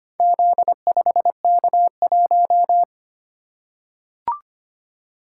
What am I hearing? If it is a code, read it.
Morse code, heard as a keyed tone, about 25 wpm, 690 Hz. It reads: Z5K1
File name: practice / 25wpm / Four Letter-Number Combinations - 25wpm